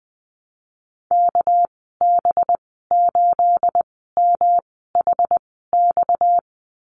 Morse code, heard as a keyed tone, about 19 wpm, 700 Hz. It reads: K B 8 M H X
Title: 238 Lesson 38g - Letter q - Call Signs 20wpm